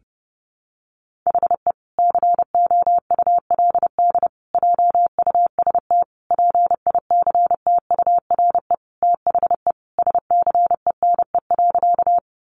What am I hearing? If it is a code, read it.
Morse code, heard as a keyed tone, about 30 wpm, 700 Hz. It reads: HE COULD JUST PICTURE THE SCENE.
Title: Sentences from Top 1100 Words 30wpm